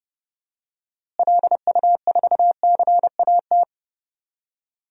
Morse code, heard as a keyed tone, about 30 wpm, 700 Hz. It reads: LU4CAT
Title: DXCC-100 Call Signs - 6 Characters - Repeated 2x - 30wpm